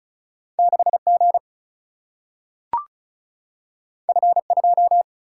Morse code, heard as a keyed tone, about 36 wpm, 700 Hz. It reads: 6G F2
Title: Single-Letter Single-Number 35wpm